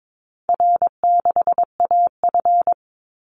Morse code, heard as a keyed tone, about 22 wpm, 700 Hz. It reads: R6AF
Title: Easy Call Signs Repeated 3x - 22wpm